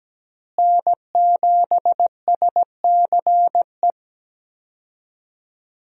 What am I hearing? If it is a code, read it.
Morse code, heard as a keyed tone, about 17 wpm, 700 Hz. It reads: N7SCE
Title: USA Call Signs - 5 Characters - Repeated 3x - 17wpm